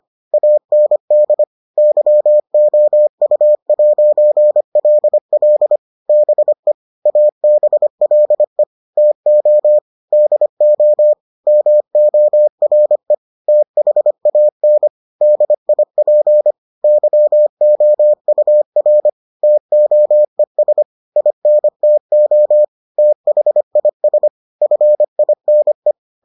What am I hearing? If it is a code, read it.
Morse code, heard as a keyed tone, about 25 wpm, 600 Hz. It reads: AND YOU'LL BE ABLE TO DO MORE THAN DIP YOUR TOES INTO THIS FINE